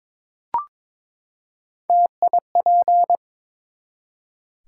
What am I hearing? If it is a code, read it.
Morse code, heard as a keyed tone, about 22 wpm, 700 Hz. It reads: TIP
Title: Three Letter Words - 22wpm